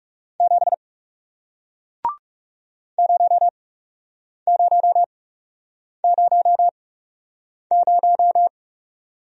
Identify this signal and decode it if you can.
Morse code, heard as a keyed tone, about 39 wpm, spaced as for 8 wpm, 700 Hz. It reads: 7 0000